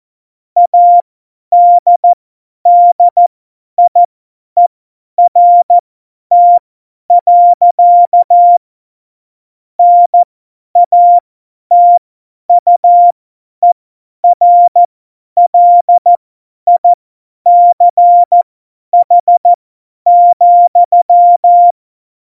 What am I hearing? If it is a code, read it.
Morse code, heard as a keyed tone, about 14 wpm, spaced as for 10 wpm, 700 Hz. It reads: ADDIERT. NATUERLICH,
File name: Morsekurs / Witze / Witz 04. Februar 2020, 50 F70